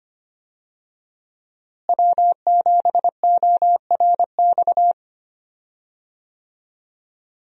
Morse code, heard as a keyed tone, about 25 wpm, 700 Hz. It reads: W7ORX
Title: USA Call Signs - 5 Characters - Repeated 2x - 25wpm